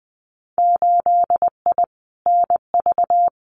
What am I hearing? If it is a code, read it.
Morse code, heard as a keyed tone, about 20 wpm, 700 Hz. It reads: 8I NV
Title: Winter Field Day Exchanges Encore 20wpm